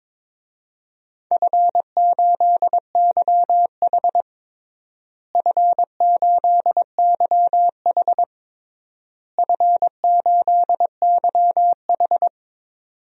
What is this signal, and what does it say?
Morse code, heard as a keyed tone, about 22 wpm, 700 Hz. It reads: F8YH F8YH F8YH
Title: DXCC-100 Call Signs - 4 Characters - Repeated 3x - 22wpm